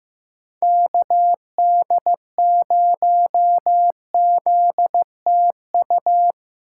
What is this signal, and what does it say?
Morse code, heard as a keyed tone, about 15 wpm, 700 Hz. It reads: KD0ZTU
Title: USA Call Signs - 3x Character Spacing - 15wpm